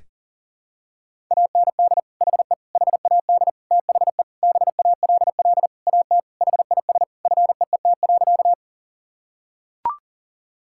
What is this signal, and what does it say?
Morse code, heard as a keyed tone, about 37 wpm, 700 Hz. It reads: AND HE HAD THE BALL AT HIS FEET.
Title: Sentences from Top 1000 Words - Review Part 6 - 40wpm